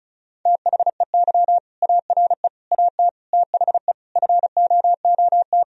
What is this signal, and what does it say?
Morse code, heard as a keyed tone, about 36 wpm, 700 Hz. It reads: THEY ARE AT THE FOOT
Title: Sentences from Top 1000 Words - Review Part 4 - 35wpm